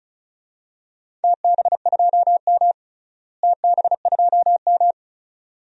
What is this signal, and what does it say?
Morse code, heard as a keyed tone, about 34 wpm, 700 Hz. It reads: TB2M TB2M
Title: DXCC-100 Call Signs - 4 Characters - Repeated 2x - 35wpm